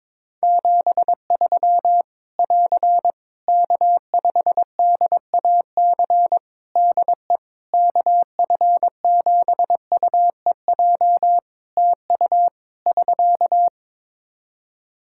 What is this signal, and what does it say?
Morse code, heard as a keyed tone, about 22 wpm, 700 Hz. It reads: 73 <AR> K5DAC DE KF7UEJ TU <SK>
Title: Practice QSO - Exchange 4 - Ending - 22wpm